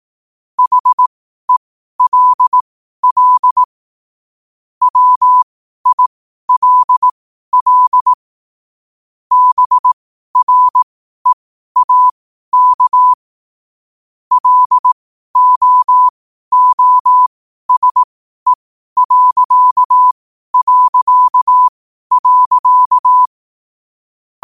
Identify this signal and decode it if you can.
Morse code, heard as a keyed tone, about 18 wpm, spaced as for 12 wpm, 1000 Hz. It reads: HELL WILL BREAK LOOSE...
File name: qotdfiles / 12wpm